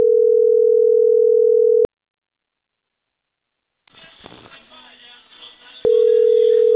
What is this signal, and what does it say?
Ringback tone (North America), 440 + 480 Hz.